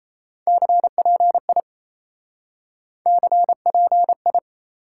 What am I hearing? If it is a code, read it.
Morse code, heard as a keyed tone, about 37 wpm, 700 Hz. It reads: CPI CPI